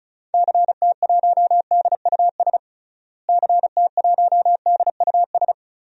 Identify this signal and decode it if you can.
Morse code, heard as a keyed tone, about 36 wpm, 700 Hz. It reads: CT1DUS CT1DUS
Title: DXCC-100 Call Signs - 6 Characters - Repeated 2x - 35wpm